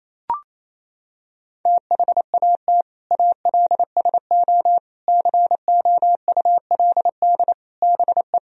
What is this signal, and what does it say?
Morse code, heard as a keyed tone, about 27 wpm, 700 Hz. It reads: THAT ALSO COULD BE
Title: Sentences from Top 1000 Words - Review Part 1 - 28wpm